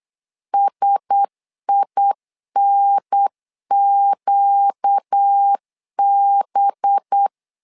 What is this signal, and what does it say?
Morse code, heard as a keyed tone, about 8 wpm, 800 Hz. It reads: SINQB
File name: LEZIONI / 5 LBVC / 5 LBVC 8wpm 2935